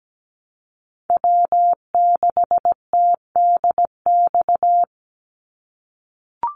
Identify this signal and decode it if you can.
Morse code, heard as a keyed tone, about 17 wpm, 700 Hz. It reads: W6TDX
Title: NA SOTA Chaser Callsigns 17wpm